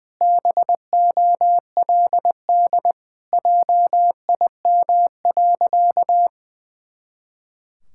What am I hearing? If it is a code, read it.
Morse code, heard as a keyed tone, about 20 wpm, 700 Hz. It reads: BOLD JIM.
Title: Pangram Challenge v2 - 20wpm